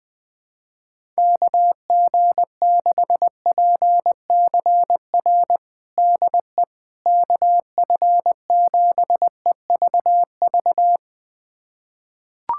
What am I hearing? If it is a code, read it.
Morse code, heard as a keyed tone, about 20 wpm, 700 Hz. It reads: KG6PCR DE KF7EVV
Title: USA Callsign DE Callsign - 6 Characters - 20wpm